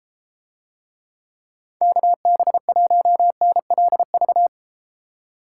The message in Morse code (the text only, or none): KB1NLV